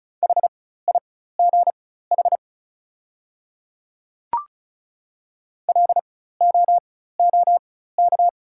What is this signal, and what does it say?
Morse code, heard as a keyed tone, about 34 wpm, spaced as for 16 wpm, 700 Hz. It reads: HIGH LOOK
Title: 4x Character Spacing - Top 500 Words - 35wpm